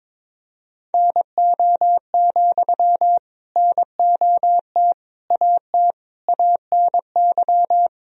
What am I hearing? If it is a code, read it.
Morse code, heard as a keyed tone, about 22 wpm, 700 Hz. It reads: NO, NOT AT ANY
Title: Sentences from Top 1000 Words - Review Part 4 - 22wpm